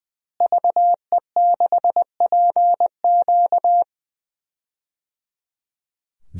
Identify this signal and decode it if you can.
Morse code, heard as a keyed tone, about 20 wpm, 700 Hz. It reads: VE6PQ
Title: DXCC-100 Call Signs - 5 Characters - Repeated 3x - 20wpm